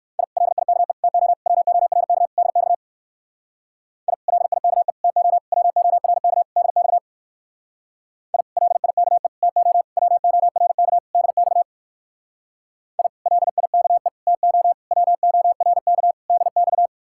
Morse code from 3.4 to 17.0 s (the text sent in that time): I LIKE TO WORK DX I LIKE TO WORK DX I LIKE TO WORK DX